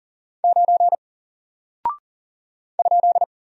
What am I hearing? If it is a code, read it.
Morse code, heard as a keyed tone, about 40 wpm, 700 Hz. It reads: 9 ?
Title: Single Character (Letter:Number:Punctuation:Prosign) 40wpm